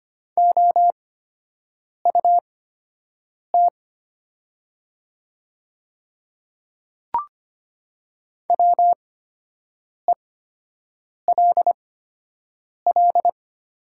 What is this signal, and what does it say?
Morse code, heard as a keyed tone, about 25 wpm, spaced as for 7 wpm, 700 Hz. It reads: OUT WELL
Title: 8x Character Spacing - Top 200 Words - 25wpm